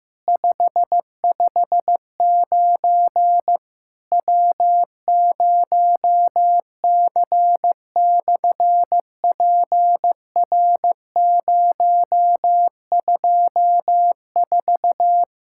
Morse code, heard as a keyed tone, about 15 wpm, 700 Hz. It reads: 559 W0C/PR024
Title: SOTA QSO Practice 15wpm